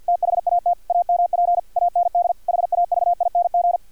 Morse code, heard as a keyed tone, about 50 wpm, 700 Hz, with radio noise. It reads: THAT AMP AND HAVING
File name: archived / CRSnet